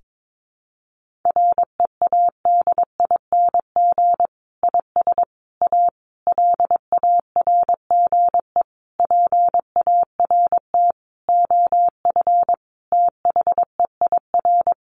Morse code, heard as a keyed tone, about 22 wpm, 700 Hz. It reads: READING IS A LARGE PART OF THEIR
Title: Sentences from Top 1400 Words 22wpm